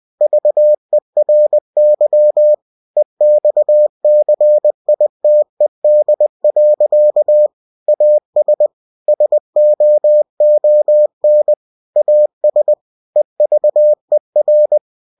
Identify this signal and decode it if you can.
Morse code, heard as a keyed tone, about 20 wpm, 600 Hz. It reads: VERY EXCITED. AS SOON AS EVER
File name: cw-books / winnie-the-pooh / winnie-20wpm-page-0106